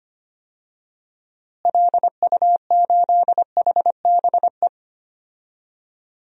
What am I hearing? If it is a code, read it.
Morse code, heard as a keyed tone, about 25 wpm, 700 Hz. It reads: LU8HBE
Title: DXCC-100 Call Signs - 6 Characters - Repeated 3x - 25wpm